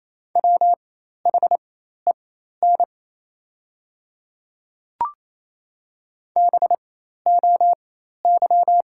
Morse code, heard as a keyed tone, about 28 wpm, spaced as for 13 wpm, 700 Hz. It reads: WHEN BOY
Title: 4x Character Spacing - Top 300 Words - 28wpm